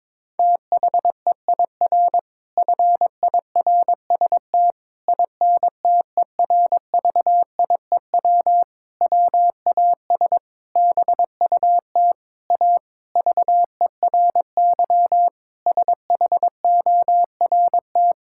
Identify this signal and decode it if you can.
Morse code, heard as a keyed tone, about 22 wpm, 700 Hz. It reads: THEIR FIRST INTERVIEW WAS BUT A VERY SHORT
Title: Sentences from Top 1600 Words 22wpm